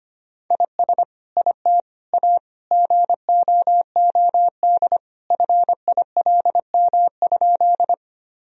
Morse code, heard as a keyed tone, about 25 wpm, 700 Hz. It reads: IS IT A GOOD FILM?
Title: Sentences from Top 1000 Words - Review Part 3 - 25wpm